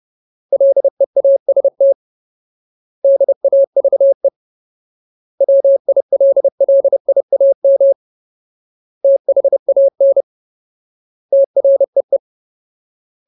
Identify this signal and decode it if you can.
Morse code, heard as a keyed tone, about 30 wpm, 550 Hz. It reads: LEAST DAVE WILLIAM THAN TREE
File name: Ultimate Word List / 30WPM Ultimate